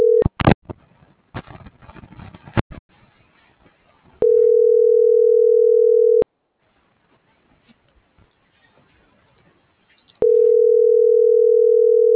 Ringback tone (North America), 440 + 480 Hz.